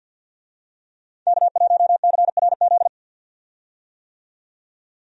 Morse code, heard as a keyed tone, about 52 wpm, 700 Hz. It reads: K1CRZ
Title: USA Call Signs - 5 Characters - 50wpm